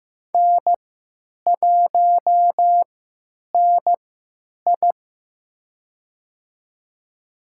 Morse code, heard as a keyed tone, about 15 wpm, spaced as for 9 wpm, 700 Hz. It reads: N1NI
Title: USA Call Signs - 4 Characters - 3x Character Spacing - 15wpm